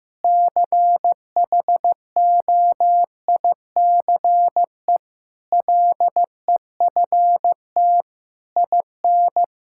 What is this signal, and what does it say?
Morse code, heard as a keyed tone, about 15 wpm, 700 Hz. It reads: CHOICE LEFT IN
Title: Sentences from Top 1000 Words - Review Part 6 - 15wpm